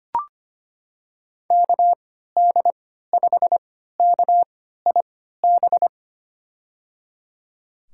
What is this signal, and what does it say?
Morse code, heard as a keyed tone, about 25 wpm, spaced as for 14 wpm, 700 Hz. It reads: KD5KIB
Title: USA Call Signs - 6 Characters - 3x Character Spacing - 25wpm